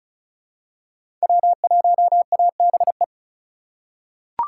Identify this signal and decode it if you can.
Morse code, heard as a keyed tone, about 36 wpm, 700 Hz. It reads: W1ABE
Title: USA Call Signs - 8x Character Spacing - 35wpm